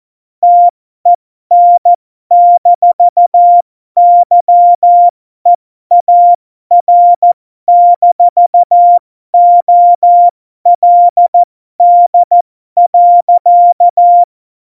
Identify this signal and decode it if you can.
Morse code, heard as a keyed tone, about 14 wpm, spaced as for 12 wpm, 700 Hz. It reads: TEN-YEAR-OLD.
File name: Morsekurs / Witze